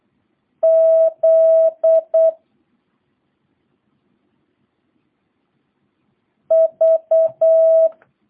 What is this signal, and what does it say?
Morse code, heard as a keyed tone, about 8 wpm, 650 Hz. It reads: Z V